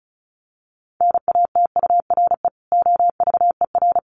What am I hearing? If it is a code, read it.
Morse code, heard as a keyed tone, about 34 wpm, 700 Hz. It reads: NATURE OVER
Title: Sentences from Top 1500 Words 35wpm